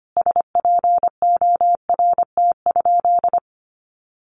Morse code, heard as a keyed tone, about 25 wpm, 700 Hz. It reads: SPORT?
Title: Sentences from Top 1500 Words 25wpm